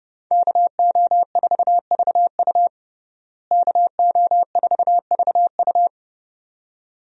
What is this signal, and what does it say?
Morse code, heard as a keyed tone, about 30 wpm, 700 Hz. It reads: KO4VU KO4VU
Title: USA Call Signs - 5 Characters - Repeated 2x - 30wpm